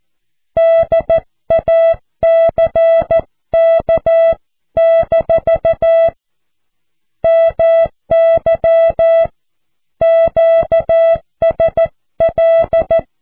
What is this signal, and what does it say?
Morse code, heard as a keyed tone, about 14 wpm, 660 Hz. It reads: DACK- MY QSL
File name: cw / QSO